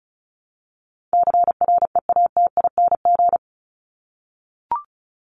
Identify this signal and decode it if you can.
Morse code, heard as a keyed tone, about 34 wpm, 700 Hz. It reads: CREATING
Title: ING Suffix - Top 100 Words with Base Words 35wpm